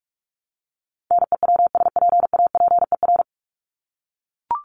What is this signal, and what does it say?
Morse code, heard as a keyed tone, about 44 wpm, 700 Hz. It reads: NEWSPAPER
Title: Top 751-1000 Words 45wpm